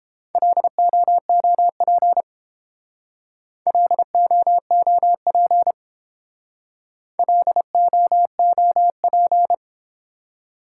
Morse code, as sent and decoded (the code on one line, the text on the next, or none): .-.. --- --- .--. / .-.. --- --- .--. / .-.. --- --- .--.
LOOP LOOP LOOP